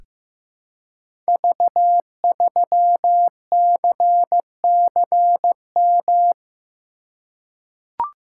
Morse code, heard as a keyed tone, about 15 wpm, 700 Hz. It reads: V3CCM
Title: Code Groups - 5 Letter:Numbers - 15wpm